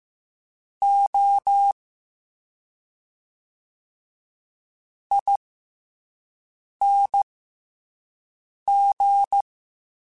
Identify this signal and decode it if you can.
Morse code, heard as a keyed tone, about 15 wpm, spaced as for 4 wpm, 800 Hz. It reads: OING